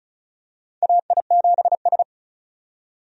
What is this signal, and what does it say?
Morse code, heard as a keyed tone, about 34 wpm, 700 Hz. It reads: AI7S
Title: USA Call Signs - 4 Characters - 35wpm